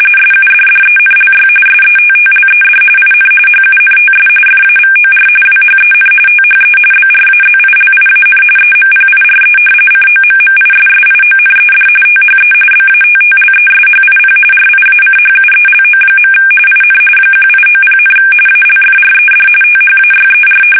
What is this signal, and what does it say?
RTTY, about 45.45 baud. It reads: SUDDENLY SETTLED IN HUNTS FAVOR WHEN THE BRAZILIAN EMPIRE COMMISSIONED THE BATTLESHIP RIACHUELO. IN 1890, ALFRED THAYER
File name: fsk850 / battleships